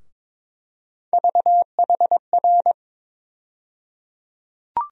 Morse code, heard as a keyed tone, about 21 wpm, 700 Hz. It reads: VHR
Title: USA Call Sign Postfixes - 3 Letters 22wpm